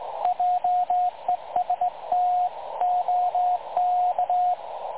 Morse code, heard as a keyed tone, about 18 wpm, 720 Hz, with radio noise. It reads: JEST OK
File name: dlazaawansowanych / nagraniazpasma / websdr recording 3565.0kHz